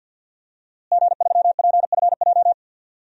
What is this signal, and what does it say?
Morse code, heard as a keyed tone, about 50 wpm, 700 Hz. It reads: G3PFJ
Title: DXCC-100 Call Signs - 5 Characters - Repeated 2x - 50wpm